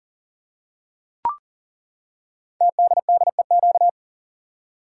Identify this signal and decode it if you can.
Morse code, heard as a keyed tone, about 38 wpm, 690 Hz. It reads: TDDEQ
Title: Code Groups - 5 Letters - 40wpm